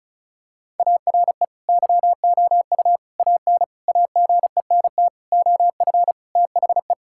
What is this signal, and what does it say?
Morse code, heard as a keyed tone, about 36 wpm, 700 Hz. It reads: ARE YOU AN AGENT OF THE
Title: Sentences from Top 1600 Words 35wpm